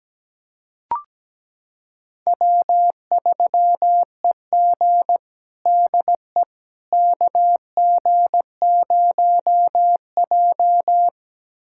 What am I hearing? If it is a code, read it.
Morse code, heard as a keyed tone, about 17 wpm, 700 Hz. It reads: W3EG DE KG0J
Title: USA Callsign DE Callsign - 4 Characters - 17wpm